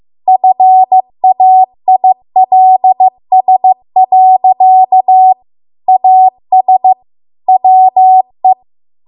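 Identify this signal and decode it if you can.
Morse code, heard as a keyed tone, about 15 wpm, 750 Hz. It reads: FAILS. AS WE
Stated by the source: Morse code practice at 15 wpm from the national amateur radio association's code archive.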